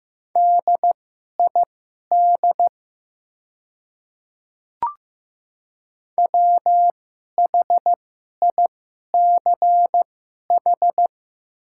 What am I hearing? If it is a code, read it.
Morse code, heard as a keyed tone, about 15 wpm, spaced as for 11 wpm, 700 Hz. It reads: DID WHICH
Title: 2x Character Spacing - Top 200 Words - 15wpm